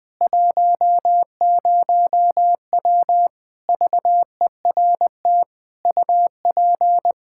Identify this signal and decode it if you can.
Morse code, heard as a keyed tone, about 20 wpm, 700 Hz. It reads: 10W VERT UP
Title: Practice QSO - Exchange 2 - Info - 20wpm